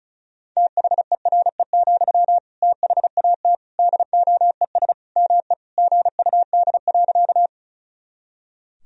Morse code, heard as a keyed tone, about 36 wpm, 700 Hz. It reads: THERE, THAT DOES ME GUD.
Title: Sentences from Top 200 Words and Abbreviations 35wpm